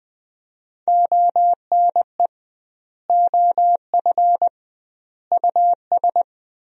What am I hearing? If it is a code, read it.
Morse code, heard as a keyed tone, about 20 wpm, 700 Hz. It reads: ONE OF US
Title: 2x Word-Spacing - Sentences from Top 300 Words - 20wpm